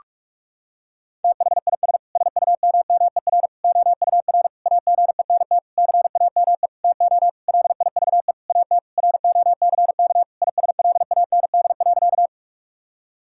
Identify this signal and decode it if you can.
Morse code, heard as a keyed tone, about 46 wpm, 700 Hz. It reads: THIS SUMMER OUR AGENT CAME TO LIVE AT ROCK ISLAND.
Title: Sentences from Top 1600 Words 45wpm